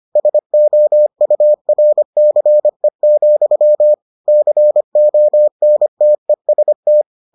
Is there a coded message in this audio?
Yes: Morse code from 0.2 to 7.0 s, heard as a keyed tone, about 25 wpm, 600 Hz. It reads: SOURCE, CONTEST